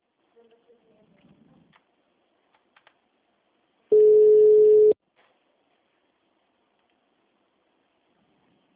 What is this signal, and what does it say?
Ringback tone (Europe, 425 Hz), with speech.